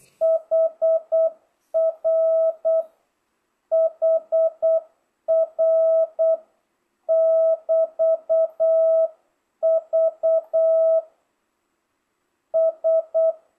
Morse code, heard as a keyed tone, about 8 wpm, 660 Hz. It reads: HR HR <BT> V S